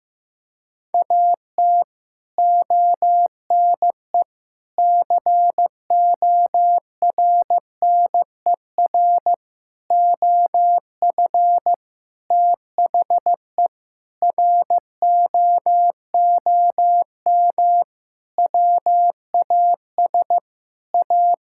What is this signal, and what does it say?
Morse code, heard as a keyed tone, about 15 wpm, 700 Hz. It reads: AT ONE CORNER OF THE ROOM WAS A
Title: Sentences from Top 1600 Words 15wpm